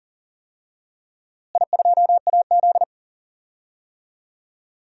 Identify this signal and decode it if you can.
Morse code, heard as a keyed tone, about 40 wpm, 700 Hz. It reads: I2AZ
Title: DXCC-100 Call Signs - 4 Characters - Repeated 2x - 40wpm